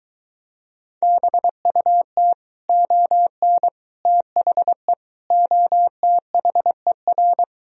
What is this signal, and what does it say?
Morse code, heard as a keyed tone, about 23 wpm, 700 Hz. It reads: BUT ON THE OTHER